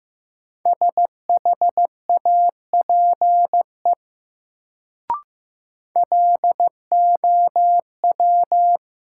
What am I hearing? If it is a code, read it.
Morse code, heard as a keyed tone, about 15 wpm, 700 Hz. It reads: SHAPE LOW
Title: Sentences from Top 500 Words - New Word Drill - 15wpm